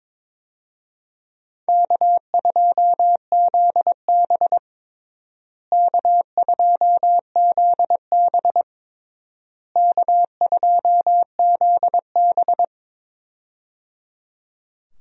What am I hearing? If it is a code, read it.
Morse code, heard as a keyed tone, about 22 wpm, 700 Hz. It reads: K2ZB K2ZB K2ZB
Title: USA Call Signs - 4 Characters - Repeated 3x - 22wpm